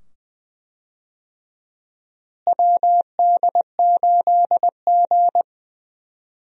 Morse code, heard as a keyed tone, about 20 wpm, 700 Hz. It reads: WD8G
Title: USA Call Signs - 4 Characters - Repeated 2x - 20wpm